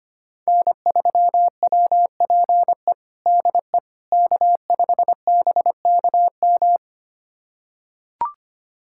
Morse code, heard as a keyed tone, about 25 wpm, 700 Hz. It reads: N3WPE DE K5BKM
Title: USA Callsign DE Callsign - 5 Characters - 25wpm